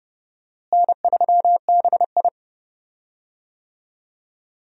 Morse code, heard as a keyed tone, about 32 wpm, 700 Hz. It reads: N3BI
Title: USA Call Signs - 4 Characters - Repeated 2x - 30wpm